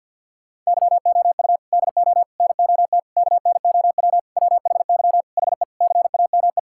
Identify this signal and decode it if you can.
Morse code, heard as a keyed tone, about 50 wpm, 700 Hz. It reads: YOU DO NOT KNOW WHY HE CAME
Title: Sentences from Top 1000 Words - Review Part 6 - 50wpm